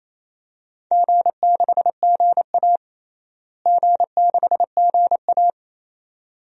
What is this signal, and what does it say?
Morse code, heard as a keyed tone, about 28 wpm, 700 Hz. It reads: G6GA G6GA
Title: DXCC-100 Call Signs - 4 Characters - Repeated 2x - 28wpm